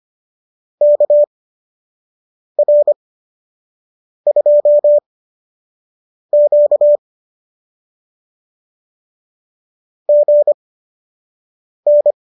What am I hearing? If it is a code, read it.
Morse code, heard as a keyed tone, about 25 wpm, spaced as for 6 wpm, 590 Hz. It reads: KR2Q GN